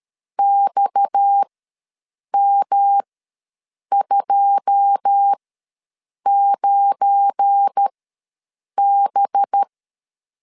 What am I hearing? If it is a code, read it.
Morse code, heard as a keyed tone, about 13 wpm, spaced as for 7 wpm, 800 Hz. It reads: XM29B
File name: LEZIONI / 9 2 ws 12wpm 3406